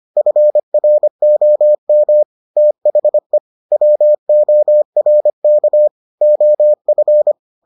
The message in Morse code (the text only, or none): FROM THE WORK OF